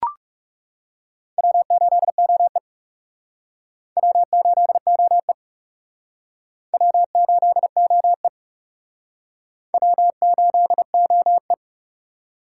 Morse code, heard as a keyed tone, about 36 wpm, 700 Hz. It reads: W8OE W8OE W8OE W8OE
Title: Speed Racing - USA 1x2 Call Signs 30wpm